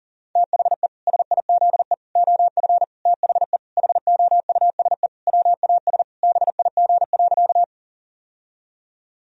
Morse code, heard as a keyed tone, about 40 wpm, 700 Hz. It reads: THE SIZE OF THE HOUSE WAS BIG.
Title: Sentences from Top 1000 Words - Review Part 2 - 40wpm